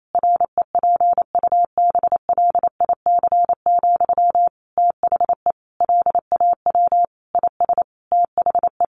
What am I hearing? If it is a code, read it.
Morse code, heard as a keyed tone, about 28 wpm, 700 Hz. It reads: REPUBLIC, THE LAW IS THE
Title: Sentences from Top 1500 Words 28wpm